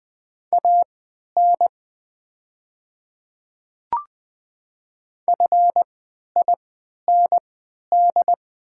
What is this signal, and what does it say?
Morse code, heard as a keyed tone, about 20 wpm, spaced as for 11 wpm, 700 Hz. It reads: AN FIND
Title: 3x Character Spacing - Top 300 Words - 20wpm